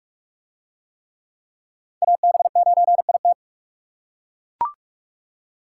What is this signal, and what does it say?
Morse code, heard as a keyed tone, about 41 wpm, 700 Hz. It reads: AB9IT